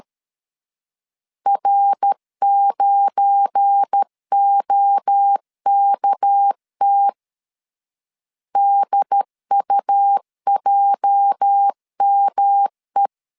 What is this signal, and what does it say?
Morse code, heard as a keyed tone, about 13 wpm, 800 Hz. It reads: R9OKT DUJME
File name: LEZIONI / 8 YJ45 / 8 YJ 12wpm 2235